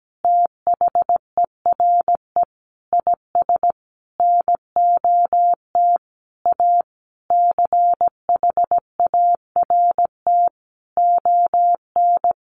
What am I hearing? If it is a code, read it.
Morse code, heard as a keyed tone, about 17 wpm, 700 Hz. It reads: THERE IS NOT A CHART ON
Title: Sentences from Top 1300 Words 17wpm